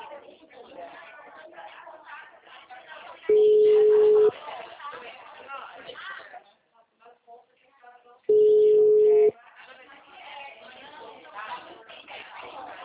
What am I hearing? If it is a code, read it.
Ringback tone (Europe, 425 Hz), with speech.
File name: out- -2025- - - .16360